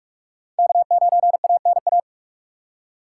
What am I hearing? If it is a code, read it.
Morse code, heard as a keyed tone, about 46 wpm, 700 Hz. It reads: K9ANA
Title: USA Call Signs - 5 Characters - 3x Character Spacing - 45wpm